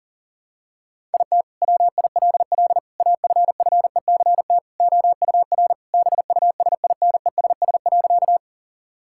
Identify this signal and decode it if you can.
Morse code, heard as a keyed tone, about 40 wpm, 700 Hz. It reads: IT WILL AFFECT OUR BUSINESS.
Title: Sentences from Top 1000 Words - Review Part 6 - 40wpm